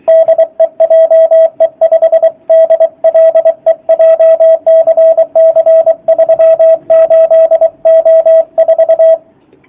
Morse code, heard as a keyed tone, about 24 wpm, 650 Hz, with radio noise. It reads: DE JE5DLE JCC 38O4
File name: jh8cbh / 2019cw / sp27- w